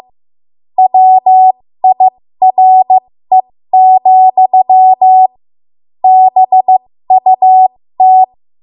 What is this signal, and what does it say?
Morse code, heard as a keyed tone, about 15 wpm, spaced as for 13 wpm, 750 Hz. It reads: WIRE, BUT
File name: Morse / 13WPM